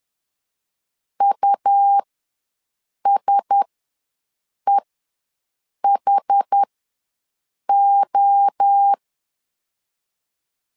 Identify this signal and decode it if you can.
Morse code, heard as a keyed tone, about 10 wpm, spaced as for 6 wpm, 800 Hz. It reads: USEHO
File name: LEZIONI / 5 LBVC / 5 LBV ws 10wpm 3645